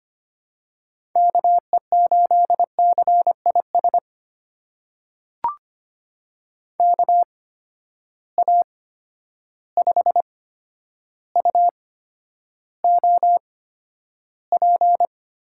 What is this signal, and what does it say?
Morse code, heard as a keyed tone, about 25 wpm, spaced as for 11 wpm, 700 Hz. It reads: KE8CIS KA5UOP
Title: USA Call Signs - 6 Characters - 8x Character Spacing - 25wpm